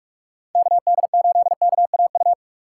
Morse code, heard as a keyed tone, about 44 wpm, 700 Hz. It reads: KD8KAU